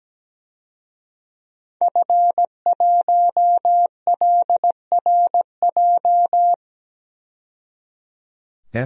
Morse code, heard as a keyed tone, about 17 wpm, 700 Hz. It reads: F1LRJ
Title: DXCC-100 Call Signs - 5 Characters - 17wpm